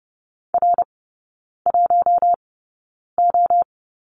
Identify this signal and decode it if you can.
Morse code, heard as a keyed tone, about 30 wpm, spaced as for 9 wpm, 700 Hz. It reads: R1O
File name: course / 30:10wpm / 087 Lesson 15e - Number 5 - Three Characters 30wpm